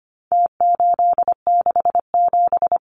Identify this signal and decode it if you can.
Morse code, heard as a keyed tone, about 25 wpm, 700 Hz. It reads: T867
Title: Numbers - Contest Style - 4-Digit 25wpm